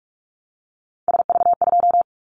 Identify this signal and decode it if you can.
Morse code, heard as a keyed tone, about 46 wpm, 700 Hz. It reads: SV2
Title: Callsign Prefixes for Top 50 DXCC - 45wpm